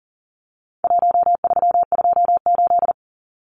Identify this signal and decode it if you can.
Morse code, heard as a keyed tone, about 40 wpm, 700 Hz. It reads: 1328
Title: Numbers - Contest Style - 4-Digit 40wpm